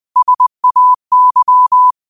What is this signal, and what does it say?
Morse code, heard as a keyed tone, about 20 wpm, 1000 Hz. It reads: SAY
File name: 20wpm